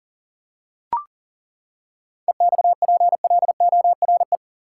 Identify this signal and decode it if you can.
Morse code, heard as a keyed tone, about 39 wpm, 700 Hz. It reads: EXPLORE
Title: Seven Letter Words - 40wpm